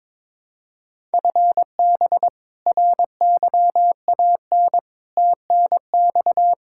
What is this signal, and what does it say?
Morse code, heard as a keyed tone, about 21 wpm, 700 Hz. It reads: FB RYAN TNX
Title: Practice QSO - Exchange 3 - More Info - 22wpm